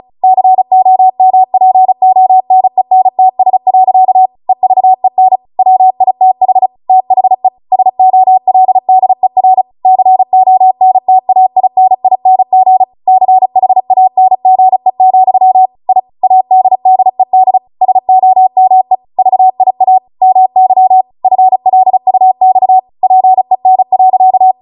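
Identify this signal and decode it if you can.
Morse code, heard as a keyed tone, about 36 wpm, 750 Hz. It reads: COMPONENTS. EVEN WITH THE SOLDER CONTAINING CHANGE, I ADDED SOME VIA MY FLUX PEN.
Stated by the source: Amateur radio Morse code practice archive transmission at 35 wpm.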